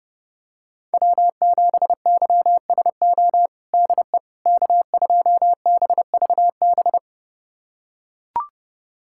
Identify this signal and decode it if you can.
Morse code, heard as a keyed tone, about 30 wpm, 700 Hz. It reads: W7YSO DE K2BVB
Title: USA Callsign DE Callsign - 5 Characters - 30wpm